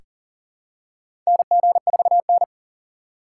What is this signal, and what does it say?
Morse code, heard as a keyed tone, about 40 wpm, 700 Hz. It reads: NG4N
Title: USA Call Signs - 4 Characters - 8x Character Spacing - 40wpm